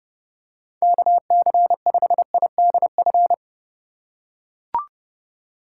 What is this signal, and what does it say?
Morse code, heard as a keyed tone, about 32 wpm, 700 Hz. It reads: KC5IDF
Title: USA Call Signs - 3x Character Spacing - 30wpm